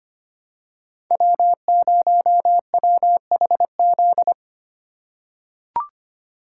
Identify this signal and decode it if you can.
Morse code, heard as a keyed tone, about 25 wpm, 700 Hz. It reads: W0WHZ
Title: USA Call Signs - 5 Characters - 2x Character Spacing - 25wpm